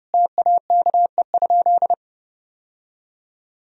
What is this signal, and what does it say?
Morse code, heard as a keyed tone, about 30 wpm, 700 Hz. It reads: TAKE?
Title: Sentences from Top 1000 Words - Review Part 1 - 30wpm